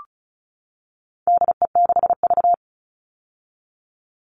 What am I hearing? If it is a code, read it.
Morse code, heard as a keyed tone, about 34 wpm, 700 Hz. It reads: DE6V
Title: All Cut Numbers - 4-Digit 35wpm